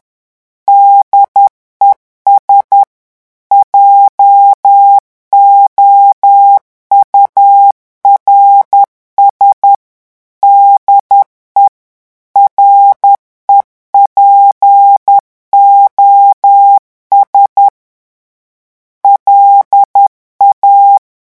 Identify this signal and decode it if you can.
Morse code, heard as a keyed tone, about 10 wpm, 800 Hz. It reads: DES JOURS DE REPOS LA
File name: revision9 citations humoristiques vitesse 10 mots minute